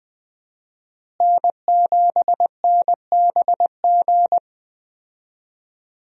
Morse code, heard as a keyed tone, about 20 wpm, 700 Hz. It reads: N7NBG
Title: USA Call Signs - 5 Characters - Repeated 2x - 20wpm